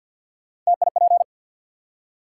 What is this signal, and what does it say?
Morse code, heard as a keyed tone, about 53 wpm, 700 Hz. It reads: TIP
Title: Three Letter Words - 50wpm